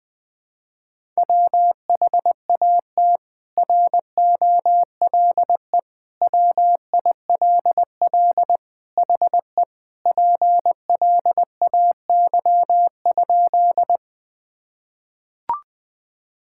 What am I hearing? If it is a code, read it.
Morse code, heard as a keyed tone, about 21 wpm, 700 Hz. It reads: WHAT ROLE WILL HE PLAY?
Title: Sentences from Top 1000 Words - Review Part 6 - 20wpm